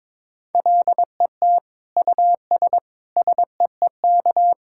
Morse code, heard as a keyed tone, about 22 wpm, 700 Hz. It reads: LET US SEEK
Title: Sentences from Top 1600 Words 22wpm